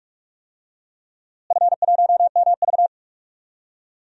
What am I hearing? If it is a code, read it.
Morse code, heard as a keyed tone, about 41 wpm, 700 Hz. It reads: F1MV
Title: DXCC-100 Call Signs - 4 Characters - Repeated 2x - 45wpm